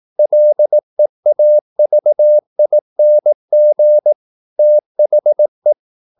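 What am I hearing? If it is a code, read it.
Morse code, heard as a keyed tone, about 18 wpm, 600 Hz. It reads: LEAVING THE
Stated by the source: Morse practice at 18 wpm, English text, from CW med Gnister.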